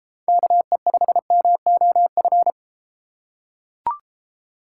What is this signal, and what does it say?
Morse code, heard as a keyed tone, about 31 wpm, 690 Hz. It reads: KE5MOF